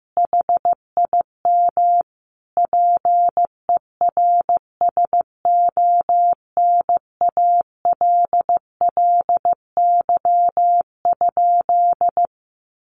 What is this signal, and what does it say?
Morse code, heard as a keyed tone, about 15 wpm, 700 Hz. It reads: HIM PERSONALLY?
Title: Sentences from Top 1400 Words 15wpm